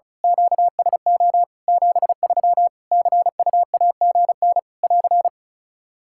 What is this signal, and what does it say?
Morse code, heard as a keyed tone, about 34 wpm, 700 Hz. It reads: QSO 73 CUAGN <AR>
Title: K1USN SST QSO Practice 35wpm